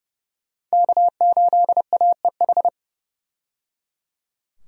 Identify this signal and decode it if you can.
Morse code, heard as a keyed tone, about 30 wpm, 700 Hz. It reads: K8AEH
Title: USA Call Signs - 5 Characters - 30wpm